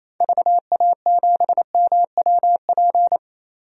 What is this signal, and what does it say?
Morse code, heard as a keyed tone, about 28 wpm, 700 Hz. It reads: VA7MWP
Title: DXCC-100 Call Signs - 6 Characters - Repeated 2x - 28wpm